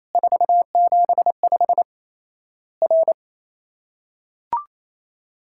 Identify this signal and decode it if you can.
Morse code, heard as a keyed tone, about 28 wpm, 700 Hz. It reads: 475 R
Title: Worked All Europe DX Contest - QTC Exchange - 28wpm